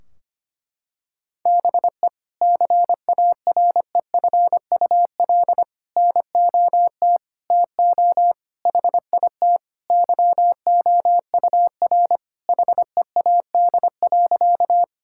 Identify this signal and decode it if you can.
Morse code, heard as a keyed tone, about 25 wpm, 700 Hz. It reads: BE CAREFUL NOT TO HIT YOUR HEAD.
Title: Sentences from Top 1600 Words 25wpm